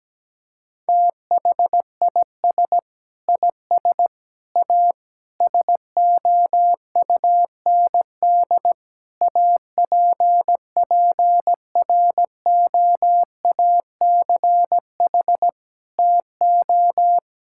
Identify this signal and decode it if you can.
Morse code, heard as a keyed tone, about 17 wpm, 700 Hz. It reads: THIS IS A SOUND APPROACH TO
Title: Sentences from Top 1000 Words - Review Part 6 - 17wpm